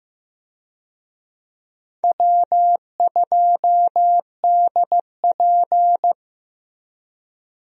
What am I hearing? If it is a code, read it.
Morse code, heard as a keyed tone, about 15 wpm, 700 Hz. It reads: W2DP
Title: USA Call Signs - 4 Characters - Repeated 3x - 15wpm